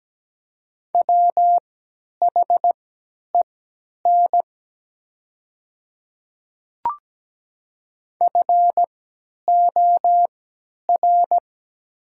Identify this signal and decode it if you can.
Morse code, heard as a keyed tone, about 17 wpm, spaced as for 10 wpm, 700 Hz. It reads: WHEN FOR
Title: 3x Character Spacing - Top 100 Words - 17wpm